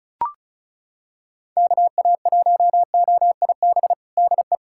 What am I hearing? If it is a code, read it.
Morse code, heard as a keyed tone, about 35 wpm, 700 Hz. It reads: KA1OIB DE
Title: USA Callsign DE Callsign 35wpm